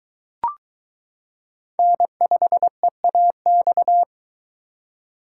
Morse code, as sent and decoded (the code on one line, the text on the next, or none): -. ..... .- -..-
N5EAX